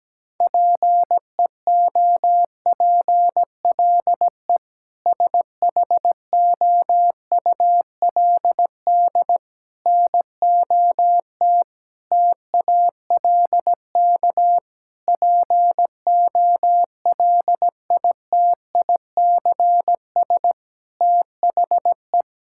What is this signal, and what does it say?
Morse code, heard as a keyed tone, about 17 wpm, 700 Hz. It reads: PEOPLE SHOULD NOT TALK POLITICS THE
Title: Sentences from Top 1000 Words - Review Part 6 - 17wpm